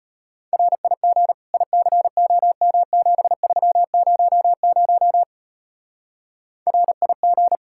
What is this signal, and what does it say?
Morse code, heard as a keyed tone, about 32 wpm, 700 Hz. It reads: RIG ICOM7300 RIG